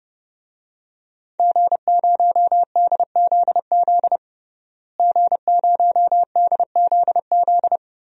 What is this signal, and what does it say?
Morse code, heard as a keyed tone, about 30 wpm, 700 Hz. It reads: G0DZZ G0DZZ
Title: DXCC-100 Call Signs - 5 Characters - Repeated 3x - 30wpm